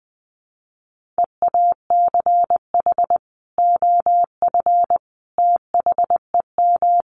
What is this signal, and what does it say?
Morse code, heard as a keyed tone, about 20 wpm, 700 Hz. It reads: EACH OF THEM
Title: Sentences from Top 1300 Words 20wpm